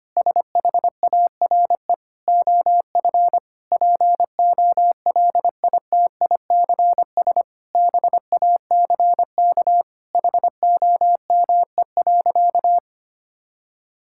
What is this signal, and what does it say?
Morse code, heard as a keyed tone, about 25 wpm, 700 Hz. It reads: SHARE OF POLITICS BACK HOME.
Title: Sentences from Top 1000 Words - Review Part 6 - 25wpm